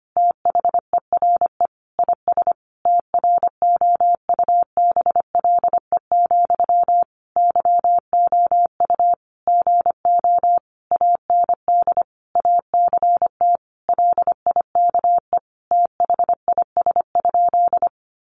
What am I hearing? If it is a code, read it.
Morse code, heard as a keyed tone, about 25 wpm, 700 Hz. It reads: THERE IS TROUBLE, YOU GO AND ACT LIKE THIS?
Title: Sentences from Top 1200 Words 25wpm